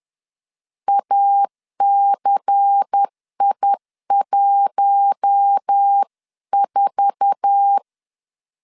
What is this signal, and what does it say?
Morse code, heard as a keyed tone, about 10 wpm, 800 Hz. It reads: ACI14
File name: CW / DET / 1 9 1500C S A 10wpm 4511